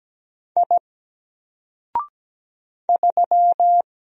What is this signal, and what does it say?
Morse code, heard as a keyed tone, about 17 wpm, 700 Hz. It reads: I 3
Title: Single Letter-Number Alt 17wpm